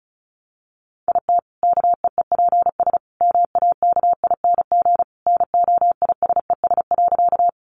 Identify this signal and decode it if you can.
Morse code, heard as a keyed tone, about 34 wpm, 700 Hz. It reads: IT KEEPS MAKING NOISES.
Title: Additional Easy Phrases 35wpm